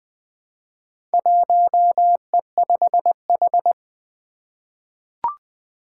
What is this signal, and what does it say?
Morse code, heard as a keyed tone, about 20 wpm, 700 Hz. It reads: 1E5H
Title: Four Challenging Letter-Number Combinations 20wpm